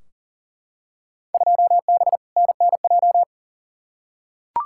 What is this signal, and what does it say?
Morse code, heard as a keyed tone, about 39 wpm, 700 Hz. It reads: 2B NNJ
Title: Field Day Exchanges Encore - 40wpm